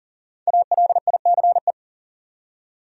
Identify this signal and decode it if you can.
Morse code, heard as a keyed tone, about 38 wpm, 700 Hz. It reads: ALICE
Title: USA Common Names - Max 5 Letters - 40wpm